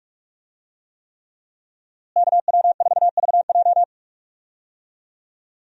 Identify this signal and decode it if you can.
Morse code, heard as a keyed tone, about 47 wpm, 700 Hz. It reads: KW4VJ
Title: USA Call Signs - 5 Characters - Repeated 3x - 45wpm